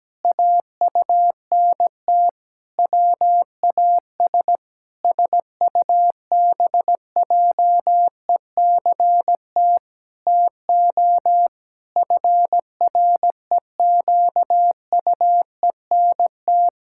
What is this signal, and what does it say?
Morse code, heard as a keyed tone, about 17 wpm, 700 Hz. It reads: AUNT WAS SUBJECT TO FREQUENT
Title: Pangram Challenge v2 - 17wpm